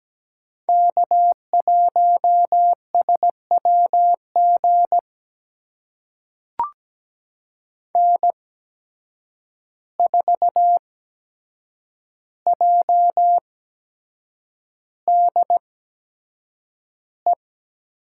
Morse code, heard as a keyed tone, about 17 wpm, spaced as for 7 wpm, 700 Hz. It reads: K1SWG N4JDE
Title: USA Call Signs - 5 Characters - 8x Character Spacing - 17wpm